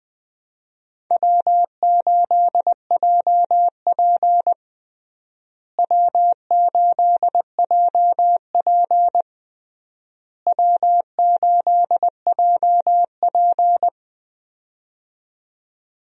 Morse code, heard as a keyed tone, about 21 wpm, 700 Hz. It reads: W8JP W8JP W8JP
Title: USA Call Signs - 4 Characters - Repeated 3x - 20wpm